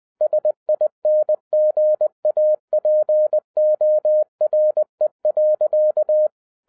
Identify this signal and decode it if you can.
Morse code, heard as a keyed tone, about 20 wpm, 610 Hz. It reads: SINGAPORE.